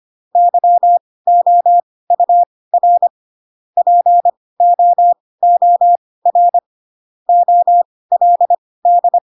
Morse code, heard as a keyed tone, about 25 wpm, spaced as for 18 wpm, 700 Hz. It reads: YOUR POOR OLD
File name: Pinocchio ch3 12 25 ik1hge